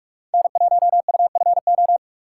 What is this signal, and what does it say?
Morse code, heard as a keyed tone, about 47 wpm, 700 Hz. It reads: N1UFO
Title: USA Call Signs - 3x Character Spacing - 45wpm